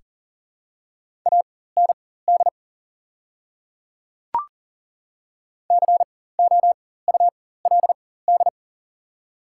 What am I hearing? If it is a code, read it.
Morse code, heard as a keyed tone, about 40 wpm, spaced as for 19 wpm, 700 Hz. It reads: AND COULD
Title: 4x Character Spacing - Top 100 Words - 40wpm